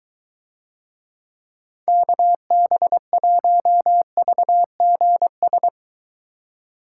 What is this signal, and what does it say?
Morse code, heard as a keyed tone, about 23 wpm, 700 Hz. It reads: KB1VGS